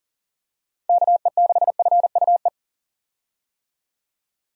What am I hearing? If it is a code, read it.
Morse code, heard as a keyed tone, about 40 wpm, 700 Hz. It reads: KE6FUE
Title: USA Call Signs - 6 Characters - 40wpm